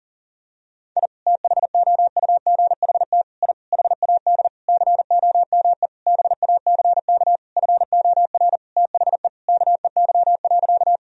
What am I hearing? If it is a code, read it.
Morse code, heard as a keyed tone, about 40 wpm, 700 Hz. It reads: I THOUGHT I HAD COME BACK FOR THE KEY.
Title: Sentences from Top 1000 Words - Review Part 4 - 40wpm